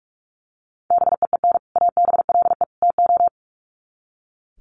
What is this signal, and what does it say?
Morse code, heard as a keyed tone, about 44 wpm, 700 Hz. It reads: BEEN ABLE TO
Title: Sets of 3 Words 45wpm